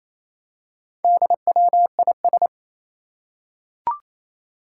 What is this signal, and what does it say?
Morse code, heard as a keyed tone, about 29 wpm, 700 Hz. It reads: DWIS
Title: Four Challenging Letter-Number Combinations 28wpm